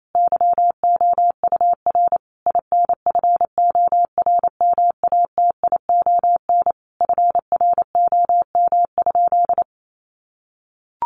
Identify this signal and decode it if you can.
Morse code, heard as a keyed tone, about 28 wpm, 700 Hz. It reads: YOUR INFORMATION FROM?
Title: Sentences from Top 600 Words 28wpm